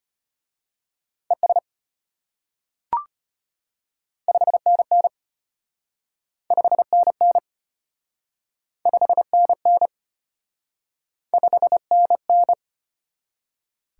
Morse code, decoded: ES 5NN 5NN 5NN 5NN

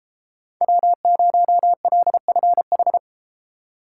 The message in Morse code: W0LFH

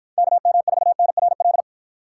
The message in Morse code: KM4NRL

